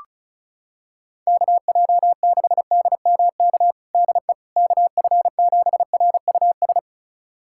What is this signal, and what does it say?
Morse code, heard as a keyed tone, about 36 wpm, 700 Hz. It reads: KJ6DMK DE KF7RUS
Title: USA Callsign DE Callsign - 6 Characters - 35wpm